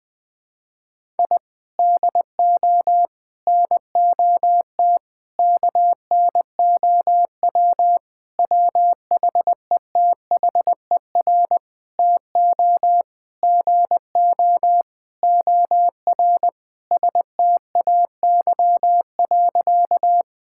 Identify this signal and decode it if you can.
Morse code, heard as a keyed tone, about 20 wpm, 700 Hz. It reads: I DO NOT KNOW WHETHER TO GO OR STAY.
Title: Sentences with 5-Word Phrases - 20wpm